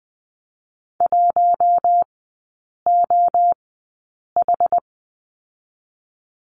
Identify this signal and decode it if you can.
Morse code, heard as a keyed tone, about 20 wpm, spaced as for 8 wpm, 700 Hz. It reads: 1OH
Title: 081 Lesson 14e - Number 2 - Three Characters 20wpm